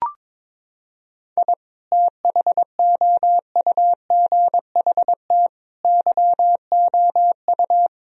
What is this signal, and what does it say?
Morse code, heard as a keyed tone, about 22 wpm, 700 Hz. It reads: I THOUGHT YOU
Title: Sentences from Top 1000 Words - Review Part 4 - 22wpm